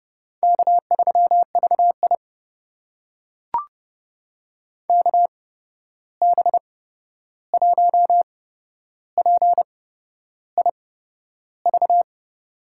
Morse code, heard as a keyed tone, about 30 wpm, spaced as for 11 wpm, 700 Hz. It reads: K3VI KB1PIV
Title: USA Call Signs - 8x Character Spacing - 30wpm